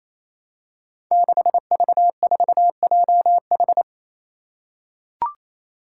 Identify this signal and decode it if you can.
Morse code, heard as a keyed tone, about 28 wpm, 700 Hz. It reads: 6V4JH
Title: Five Challenging Letter-Number Combinations 28wpm